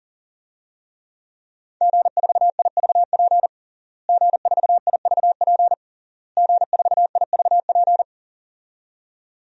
Morse code, heard as a keyed tone, about 40 wpm, 700 Hz. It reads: G4IVP G4IVP G4IVP
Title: DXCC-100 Call Signs - 5 Characters - Repeated 3x - 40wpm